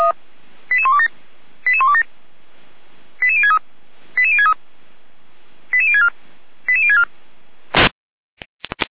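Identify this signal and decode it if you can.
Five-tone selective call (ZVEI), heard as quick runs of five tones, tones of about 70 ms. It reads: 88127 88127 88063 88063 88064 88064
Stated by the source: ZVEI five-tone selective call 88199 received with an SDR, Feuerwehr Orlamünde.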